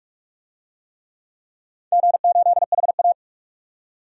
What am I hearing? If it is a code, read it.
Morse code, heard as a keyed tone, about 46 wpm, 700 Hz. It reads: G8HA
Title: DXCC-100 Call Signs - 4 Characters - 45wpm